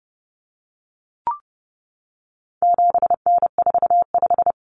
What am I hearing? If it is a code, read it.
Morse code, heard as a keyed tone, about 30 wpm, 700 Hz. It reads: 7N45
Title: Common Cut Numbers - 4-Digit 30wpm